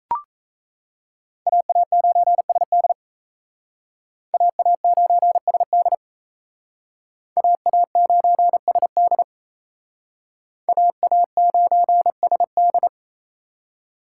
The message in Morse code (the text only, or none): AA9SD AA9SD AA9SD AA9SD